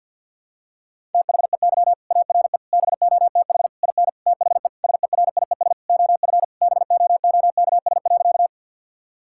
Morse code, heard as a keyed tone, about 50 wpm, 700 Hz. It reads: THEY ARE BOTH IN THE SERIES OF BOOKS.
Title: Sentences from Top 1000 Words - Review Part 5 - 50wpm